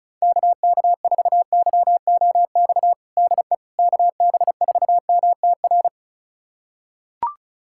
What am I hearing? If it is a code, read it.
Morse code, heard as a keyed tone, about 36 wpm, 690 Hz. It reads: KK4YOX DE KB4MTR
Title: USA Callsign DE Callsign 35wpm